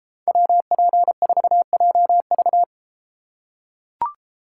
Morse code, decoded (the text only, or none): WP4JV